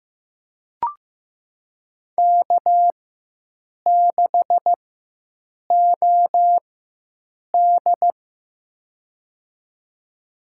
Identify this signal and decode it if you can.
Morse code, heard as a keyed tone, about 15 wpm, spaced as for 7 wpm, 700 Hz. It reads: K6OD
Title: USA Call Signs - 4x Character Spacing - 15wpm